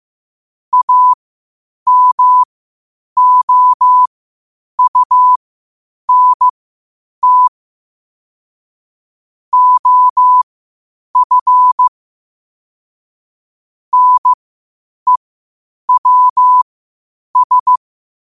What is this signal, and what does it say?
Morse code, heard as a keyed tone, about 15 wpm, spaced as for 8 wpm, 1000 Hz. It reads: AMOUNT OF NEWS
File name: qotdfiles / 8wpm